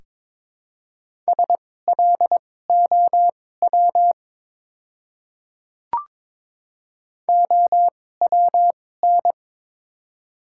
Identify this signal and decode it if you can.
Morse code, heard as a keyed tone, about 22 wpm, spaced as for 16 wpm, 700 Hz. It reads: SLOW OWN
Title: 2x Character Spacing - Top 500 Words - 22wpm